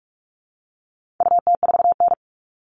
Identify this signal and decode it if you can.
Morse code, heard as a keyed tone, about 44 wpm, 700 Hz. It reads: UT4N
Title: All Cut Numbers - 4-Digit 45wpm